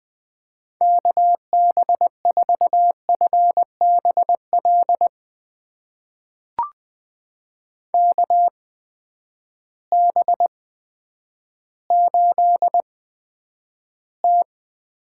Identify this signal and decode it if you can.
Morse code, heard as a keyed tone, about 20 wpm, spaced as for 10 wpm, 700 Hz. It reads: KB4FBL KB8T
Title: USA Call Signs - 6 Characters - 8x Character Spacing - 20wpm